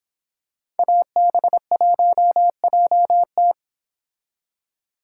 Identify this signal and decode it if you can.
Morse code, heard as a keyed tone, about 25 wpm, 700 Hz. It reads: AB1JT
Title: Speed Racing - USA 2x2 Call Signs 22wpm